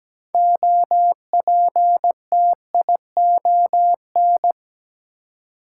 Morse code, heard as a keyed tone, about 17 wpm, 700 Hz. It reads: OPTION
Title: Six Letter Words - 17wpm